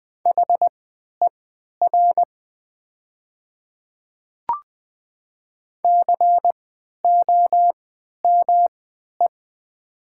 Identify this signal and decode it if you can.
Morse code, heard as a keyed tone, about 20 wpm, spaced as for 11 wpm, 700 Hz. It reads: HER COME
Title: 3x Character Spacing - Top 100 Words - 20wpm